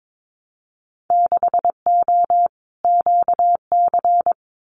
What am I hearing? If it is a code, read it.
Morse code, heard as a keyed tone, about 22 wpm, 700 Hz. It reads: 6O QC
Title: Winter Field Day Exchanges Encore 22wpm